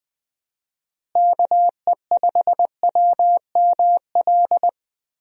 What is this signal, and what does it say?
Morse code, heard as a keyed tone, about 20 wpm, 700 Hz. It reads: KE5WML
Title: USA Call Signs - 6 Characters - Repeated 2x - 20wpm